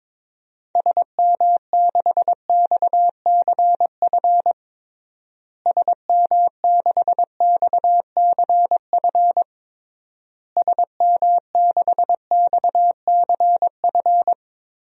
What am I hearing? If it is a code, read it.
Morse code, heard as a keyed tone, about 22 wpm, 700 Hz. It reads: SM6XCF SM6XCF SM6XCF
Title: DXCC-100 Call Signs - 6 Characters - Repeated 3x - 22wpm